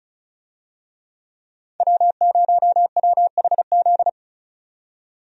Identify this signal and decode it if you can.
Morse code, heard as a keyed tone, about 36 wpm, 700 Hz. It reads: W0WHZ
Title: USA Call Signs - 5 Characters - 35wpm